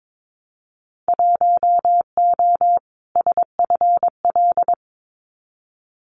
Morse code, heard as a keyed tone, about 22 wpm, 700 Hz. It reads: 1O SFL
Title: Winter Field Day Exchanges 22wpm